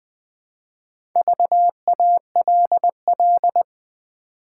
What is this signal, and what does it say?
Morse code, heard as a keyed tone, about 20 wpm, 700 Hz. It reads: VALL
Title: Build A Word - 2 Letters At A Time - 6-Letter Words - 20wpm